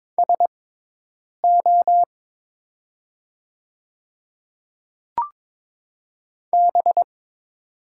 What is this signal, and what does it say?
Morse code, heard as a keyed tone, about 22 wpm, spaced as for 8 wpm, 700 Hz. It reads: SO B
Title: 6x Character Spacing - Top 300 Words - 22wpm